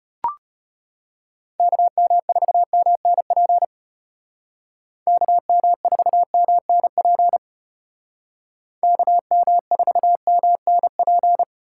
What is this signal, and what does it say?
Morse code, heard as a keyed tone, about 34 wpm, 700 Hz. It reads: KM4MNP KM4MNP KM4MNP